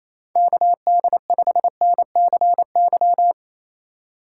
Morse code, heard as a keyed tone, about 33 wpm, 700 Hz. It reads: KD5NCY
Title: Speed Racing - USA Call Signs 28wpm